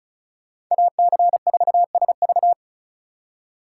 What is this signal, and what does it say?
Morse code, heard as a keyed tone, about 43 wpm, 700 Hz. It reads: AC4SV
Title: Speed Racing - USA 2x2 Call Signs 35wpm